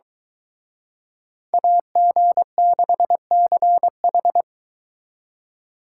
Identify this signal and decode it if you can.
Morse code, heard as a keyed tone, about 23 wpm, 700 Hz. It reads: AG6CH